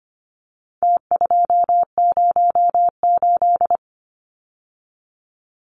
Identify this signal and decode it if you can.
Morse code, heard as a keyed tone, about 25 wpm, 700 Hz. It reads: T208
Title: Numbers - Contest Style - 4-Digit 25wpm